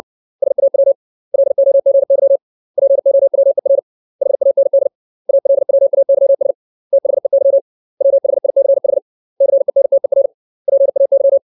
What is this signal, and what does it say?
Morse code, heard as a keyed tone, about 60 wpm, 550 Hz. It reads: SAW LOGY POWER HAND ALWAYS THEY WHICH CENTER PAY
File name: 632 words / 632 Words 60WPM